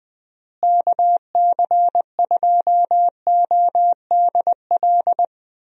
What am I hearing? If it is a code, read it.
Morse code, heard as a keyed tone, about 20 wpm, 700 Hz. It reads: KC2ODL
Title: USA Call Signs - 6 Characters - Repeated 2x - 20wpm